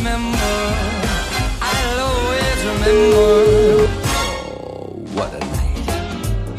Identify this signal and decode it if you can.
Ringback tone (Europe, 425 Hz), with speech.